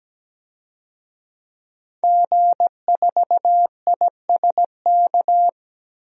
Morse code, heard as a keyed tone, about 17 wpm, 700 Hz. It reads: G4ISK
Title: DXCC-100 Call Signs - 5 Characters - 17wpm